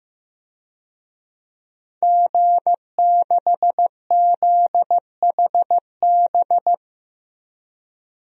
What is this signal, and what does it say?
Morse code, heard as a keyed tone, about 15 wpm, 700 Hz. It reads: G6ZHB
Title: DXCC-100 Call Signs - 5 Characters - 15wpm